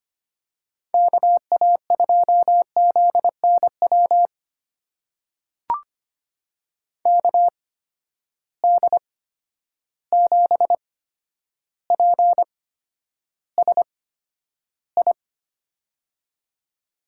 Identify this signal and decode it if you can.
Morse code, heard as a keyed tone, about 25 wpm, spaced as for 11 wpm, 700 Hz. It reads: KA2ZNW KD7PSI
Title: USA Call Signs - 6 Characters - 8x Character Spacing - 25wpm